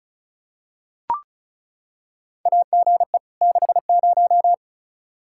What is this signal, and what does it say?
Morse code, heard as a keyed tone, about 34 wpm, 700 Hz. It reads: AGE 60
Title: Practice QSO - Exchange 3 - More Info - 35wpm